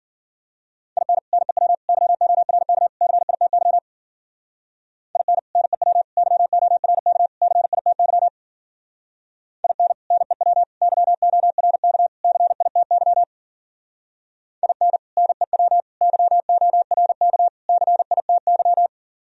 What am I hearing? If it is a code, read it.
Morse code, heard as a keyed tone, about 51 wpm, 700 Hz. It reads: IN NEW YORK CITY IN NEW YORK CITY IN NEW YORK CITY IN NEW YORK CITY